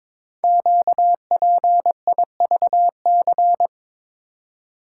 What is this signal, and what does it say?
Morse code, heard as a keyed tone, about 22 wpm, 700 Hz. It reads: QPIVC
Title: Code Groups - 5 Letters - 22wpm